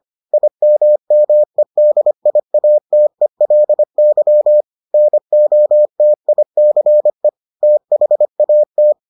Morse code, heard as a keyed tone, about 25 wpm, 600 Hz. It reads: IMMEDIATELY NOTICE THAT